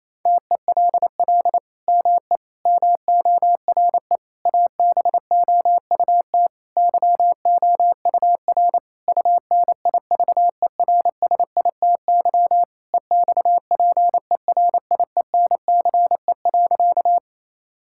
Morse code, heard as a keyed tone, about 28 wpm, 700 Hz. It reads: TELL ME MORE ABOUT YOUR UNIVERSITY EXPERIENCE.
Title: Sentences from Top 1000 Words - Review Part 5 - 28wpm